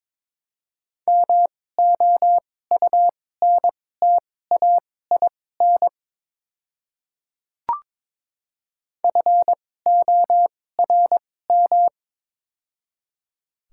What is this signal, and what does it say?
Morse code, heard as a keyed tone, about 22 wpm, spaced as for 16 wpm, 700 Hz. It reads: MOUNTAIN FORM
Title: 2x Character Spacing - Top 300 Words - 22wpm